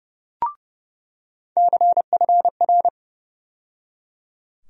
Morse code, heard as a keyed tone, about 30 wpm, 700 Hz. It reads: CFR
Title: USA Call Sign Postfixes - 3 Letters 30wpm